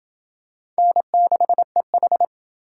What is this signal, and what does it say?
Morse code, heard as a keyed tone, about 26 wpm, 700 Hz. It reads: N6EH